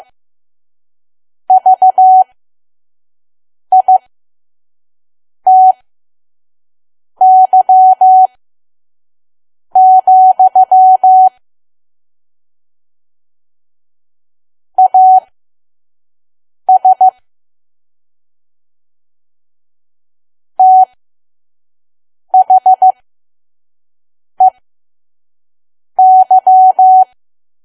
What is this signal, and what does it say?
Morse code, heard as a keyed tone, about 15 wpm, spaced as for 5 wpm, 750 Hz. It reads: VITY, AS THEY
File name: Archive / 5 WPM files / PM